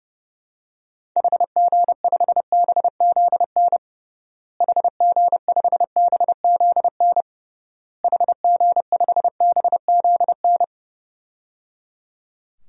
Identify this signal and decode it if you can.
Morse code, heard as a keyed tone, about 30 wpm, 700 Hz. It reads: HG5BZN HG5BZN HG5BZN
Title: DXCC-100 Call Signs - 6 Characters - Repeated 3x - 30wpm